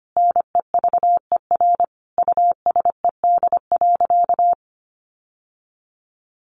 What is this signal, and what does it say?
Morse code, heard as a keyed tone, about 24 wpm, 700 Hz. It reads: NEVER USED.
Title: Sentences from Top 600 Words 25wpm